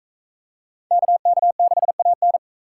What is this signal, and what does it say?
Morse code, heard as a keyed tone, about 38 wpm, 700 Hz. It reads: KK6AN